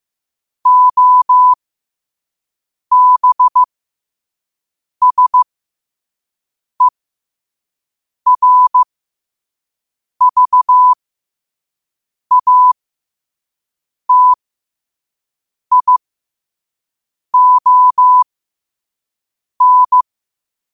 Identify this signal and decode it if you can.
Morse code, heard as a keyed tone, about 15 wpm, spaced as for 5 wpm, 1000 Hz. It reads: OBSERVATION
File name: qotdfiles / 5wpm